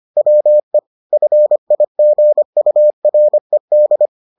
Morse code, heard as a keyed tone, about 25 wpm, 600 Hz. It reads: WE FIGURED